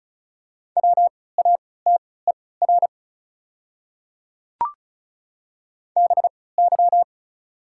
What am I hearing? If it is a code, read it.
Morse code, heard as a keyed tone, about 34 wpm, spaced as for 20 wpm, 700 Hz. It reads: WATER BY
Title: 3x Character Spacing - Top 100 Words - 35wpm